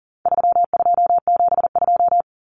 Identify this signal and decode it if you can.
Morse code, heard as a keyed tone, about 40 wpm, 700 Hz. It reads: 3272
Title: Common Cut Numbers - 4-Digit 40wpm